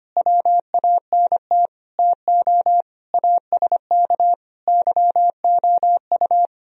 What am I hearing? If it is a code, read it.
Morse code, heard as a keyed tone, about 29 wpm, 700 Hz. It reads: WANT TO ASK YOU
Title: Speed-Racing - Sets of 5 Words 25wpm